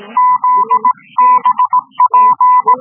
Morse code, heard as a keyed tone, about 18 wpm, 1000 Hz. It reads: ZBP